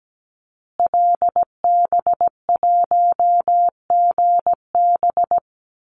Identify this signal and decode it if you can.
Morse code, heard as a keyed tone, about 17 wpm, 700 Hz. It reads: LB1GB
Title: Call Signs Repeated 3x - 17wpm